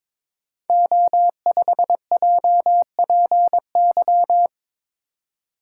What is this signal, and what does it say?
Morse code, heard as a keyed tone, about 22 wpm, 700 Hz. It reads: O5JPY
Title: Code Groups - 5 Letter:Numbers - 22wpm